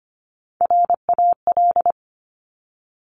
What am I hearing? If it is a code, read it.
Morse code, heard as a keyed tone, about 25 wpm, 700 Hz. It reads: RAL
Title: Top Three Letter Combinations in Words 25wpm